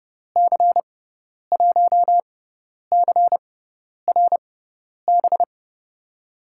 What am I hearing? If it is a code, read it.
Morse code, heard as a keyed tone, about 32 wpm, spaced as for 10 wpm, 700 Hz. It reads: C1CRB
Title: USA Call Signs - 6 Characters - 6x Character Spacing - 30wpm